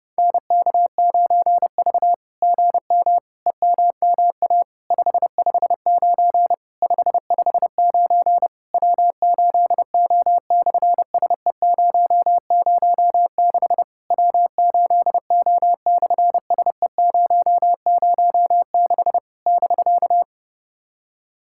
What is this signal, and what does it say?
Morse code, heard as a keyed tone, about 30 wpm, 700 Hz. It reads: NK9V GM EMMA 559 559 W8O/SE006 W8O/SE006 <BK>
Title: SOTA QSO Practice 30wpm